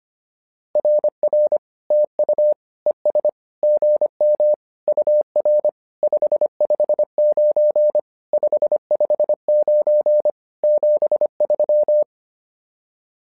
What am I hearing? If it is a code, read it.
Morse code, heard as a keyed tone, about 25 wpm, 600 Hz. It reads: RR TU ES GM UR 559 559 73
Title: SOTA QSO Practice 25wpm